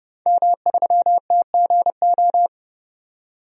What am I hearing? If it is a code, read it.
Morse code, heard as a keyed tone, about 30 wpm, 700 Hz. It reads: M3TGO
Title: DXCC-100 Call Signs - 5 Characters - 30wpm